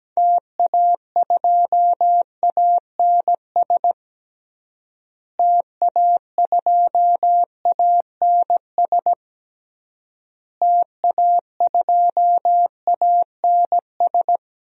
Morse code, heard as a keyed tone, about 17 wpm, 700 Hz. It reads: TA2ANS TA2ANS TA2ANS
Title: DXCC-100 Call Signs - 6 Characters - Repeated 3x - 17wpm